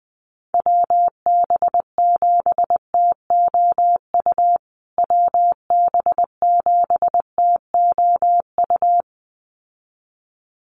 Morse code, heard as a keyed tone, about 20 wpm, 700 Hz. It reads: WB7TOU WB7TOU
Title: USA Call Signs Repeated 2x 20wpm